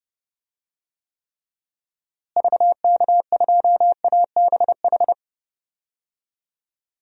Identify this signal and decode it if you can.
Morse code, heard as a keyed tone, about 30 wpm, 700 Hz. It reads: VK2ABH
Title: DXCC-100 Call Signs - 6 Characters - Repeated 2x - 30wpm